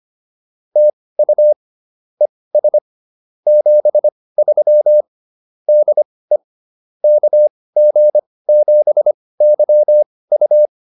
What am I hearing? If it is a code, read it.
Morse code, heard as a keyed tone, about 25 wpm, spaced as for 18 wpm, 600 Hz. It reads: TU ES 73 DE KG7YU
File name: POTA108 18